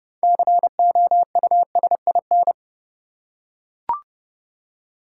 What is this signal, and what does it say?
Morse code, heard as a keyed tone, about 31 wpm, 700 Hz. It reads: COUSIN
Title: Six Letter Words - 30wpm